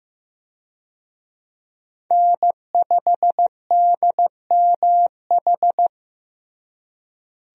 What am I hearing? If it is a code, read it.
Morse code, heard as a keyed tone, about 16 wpm, 700 Hz. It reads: N5DMH
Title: USA Call Signs - 5 Characters - Repeated 2x - 15wpm